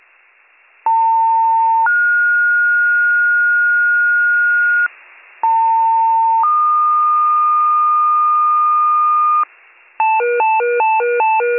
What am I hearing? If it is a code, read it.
Two-tone page, heard as a one-second tone, then a three-second tone, about 900 then 1430 Hz, about 900 then 1220 Hz, with noise. It is heard from 0.9 to 9.4 s.